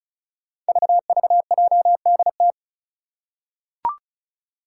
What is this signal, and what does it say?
Morse code, heard as a keyed tone, about 36 wpm, 700 Hz. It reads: VVJDT
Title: Five Challenging Letter-Number Combinations 35wpm